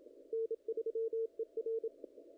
Morse code, heard as a keyed tone, about 27 wpm, 450 Hz, with radio noise. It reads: N3ER
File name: contests / ss-cw-k5tr-op-n5rz -0000